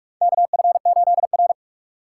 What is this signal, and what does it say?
Morse code, heard as a keyed tone, about 44 wpm, 700 Hz. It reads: KF8R
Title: USA Call Signs - 4 Characters - 8x Character Spacing - 45wpm